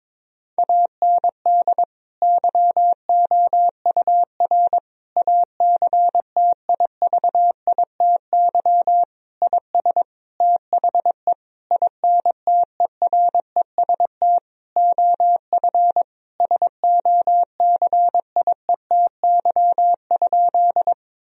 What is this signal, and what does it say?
Morse code, heard as a keyed tone, about 22 wpm, 700 Hz. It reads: AND YOUR ACTIVITY IS THE INTEREST OF SOCIETY?
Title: Sentences from Top 1000 Words - Review Part 4 - 22wpm